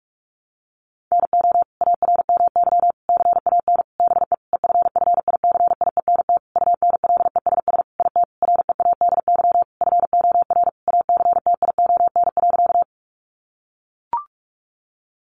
Morse code, heard as a keyed tone, about 44 wpm, 700 Hz. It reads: NO ARMY CAN BE EFFICIENT UNLESS IT READY FOR ACTION.
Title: Sentences from Top 1300 Words 45wpm